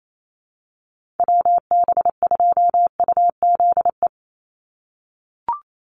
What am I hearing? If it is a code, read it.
Morse code, heard as a keyed tone, about 28 wpm, 700 Hz. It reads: WB2UZE
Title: Call Signs - 28wpm